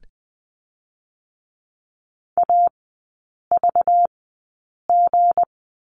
Morse code, heard as a keyed tone, about 20 wpm, spaced as for 8 wpm, 700 Hz. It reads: AVG
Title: 191 Lesson 31e - Letter v - Three Characters 20wpm